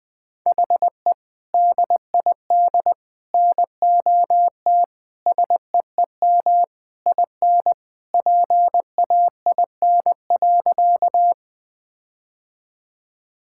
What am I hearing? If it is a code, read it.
Morse code, heard as a keyed tone, about 21 wpm, 700 Hz. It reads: HE DID NOT SEEM IN PAIN.
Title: Sentences from Top 1600 Words 20wpm